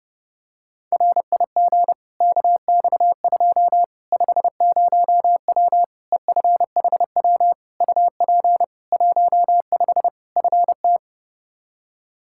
Morse code, heard as a keyed tone, about 32 wpm, 700 Hz. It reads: RIG KX2 50W EFHW UP 15 FT
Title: K1USN SST QSO Practice 30wpm